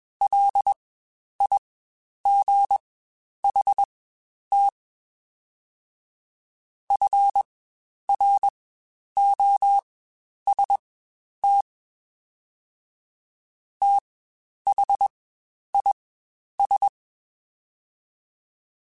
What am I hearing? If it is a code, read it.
Morse code, heard as a keyed tone, about 21 wpm, spaced as for 10 wpm, 800 Hz. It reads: LIGHT FROST THIS
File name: Spaced Characters / Spaced Characters-mp3s / Weather 4 24 sp @20